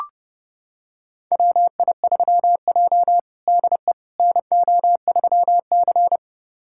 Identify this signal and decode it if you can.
Morse code, heard as a keyed tone, about 30 wpm, 700 Hz. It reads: WI3J DE NO3C
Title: USA Callsign DE Callsign - 4 Characters - 30wpm